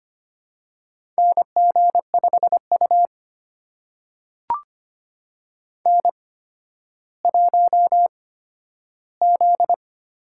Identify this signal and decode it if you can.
Morse code, heard as a keyed tone, about 25 wpm, spaced as for 12 wpm, 700 Hz. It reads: NG5U N1Z
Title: USA Call Signs - 4 Characters - 8x Character Spacing - 25wpm